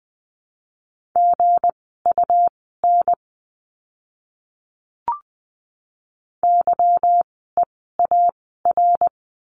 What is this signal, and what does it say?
Morse code, heard as a keyed tone, about 20 wpm, spaced as for 14 wpm, 700 Hz. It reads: GUN YEAR
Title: 160 Lesson 26f - Letter g - Words 20wpm